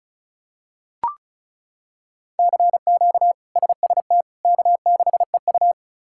Morse code, heard as a keyed tone, about 36 wpm, 700 Hz. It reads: CQ SST K6EU
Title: K1USN SST QSO Practice - 35wpm